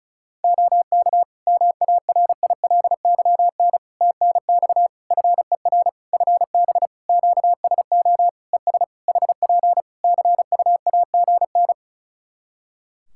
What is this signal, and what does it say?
Morse code, heard as a keyed tone, about 34 wpm, 700 Hz. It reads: OK MARILYN TNX FER FB QSO ES HP CUAGN
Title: Practice QSO - Exchange 4 - Ending - 35wpm